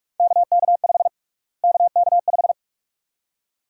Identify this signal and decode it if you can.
Morse code, heard as a keyed tone, about 46 wpm, 700 Hz. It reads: KK5 KK5
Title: USA Call Sign Prefixes - Repeated 2x - 45wpm